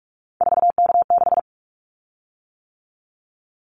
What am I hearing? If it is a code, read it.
Morse code, heard as a keyed tone, about 44 wpm, 700 Hz. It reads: 4K6
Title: Callsign Prefixes for Top 100 DXCC - 45wpm